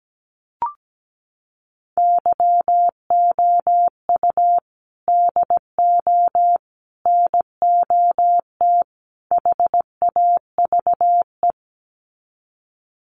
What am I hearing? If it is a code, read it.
Morse code, heard as a keyed tone, about 17 wpm, 700 Hz. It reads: YOU DO NOT HAVE
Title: Sets of 4 Words 17wpm